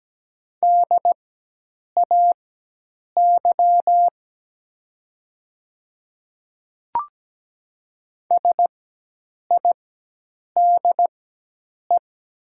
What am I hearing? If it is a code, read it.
Morse code, heard as a keyed tone, about 17 wpm, spaced as for 8 wpm, 700 Hz. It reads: DAY SIDE
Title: 4x Character Spacing - Top 500 Words - 17wpm